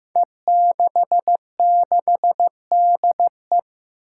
Morse code, heard as a keyed tone, about 15 wpm, 700 Hz. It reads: E66DE
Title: Five Challenging Letter-Number Combinations 15wpm